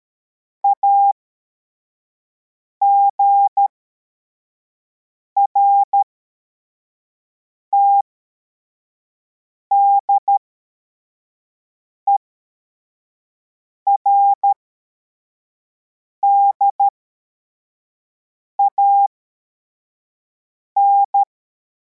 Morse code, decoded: AGRTD ERDAN